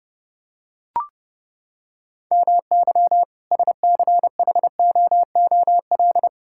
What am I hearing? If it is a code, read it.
Morse code, heard as a keyed tone, about 32 wpm, 700 Hz. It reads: MY SCHOOL
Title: Sentences from Top 1000 Words - Review Part 6 - 30wpm